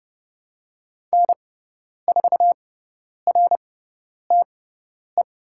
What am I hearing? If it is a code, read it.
Morse code, heard as a keyed tone, about 30 wpm, spaced as for 10 wpm, 700 Hz. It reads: N4RTE
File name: Lesson 17g - Letter u - Call Signs 30wpm